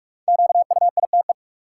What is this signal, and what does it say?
Morse code, heard as a keyed tone, about 46 wpm, 700 Hz. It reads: QUITE
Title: Five Letter Words - 45wpm